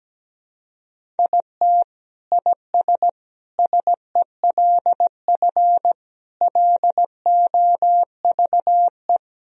Morse code, heard as a keyed tone, about 17 wpm, 700 Hz. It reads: IT IS SELF LOVE
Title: Sentences from Top 1000 Words - Review Part 1 - 17wpm